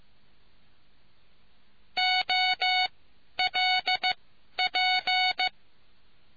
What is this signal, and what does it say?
Morse code, heard as a keyed tone, about 15 wpm, spaced as for 11 wpm, 740 Hz. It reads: OLP